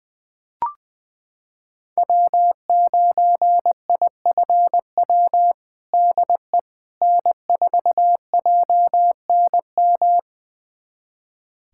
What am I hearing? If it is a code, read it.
Morse code, heard as a keyed tone, about 21 wpm, 700 Hz. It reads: W9IFW DE N4JNM
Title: USA Callsign DE Callsign - 5 Characters - 20wpm